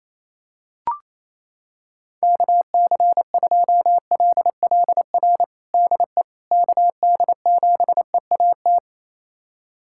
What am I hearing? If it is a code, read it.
Morse code, heard as a keyed tone, about 28 wpm, 700 Hz. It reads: KC2LLR DE KD7EAT
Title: USA Callsign DE Callsign - 6 Characters - 28wpm